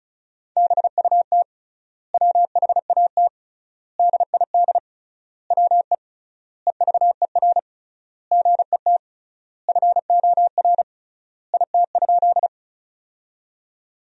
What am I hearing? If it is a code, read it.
Morse code, heard as a keyed tone, about 36 wpm, 700 Hz. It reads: BUT WHAT DID WE EVER GET FOR IT?
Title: 3x Word-Spacing - Sentences from Top 400 Words - 35wpm